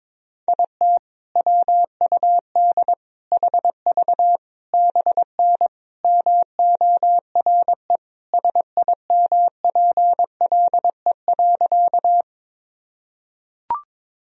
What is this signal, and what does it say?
Morse code, heard as a keyed tone, about 22 wpm, 700 Hz. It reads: IT WUD HV BN MORE SIMPLE.
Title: Sentences from Top 400 Words and Abbreviations 22wpm